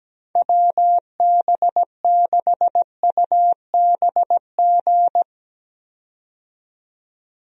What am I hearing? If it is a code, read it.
Morse code, heard as a keyed tone, about 17 wpm, 700 Hz. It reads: WB6UBG
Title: USA Call Signs - 6 Characters - 17wpm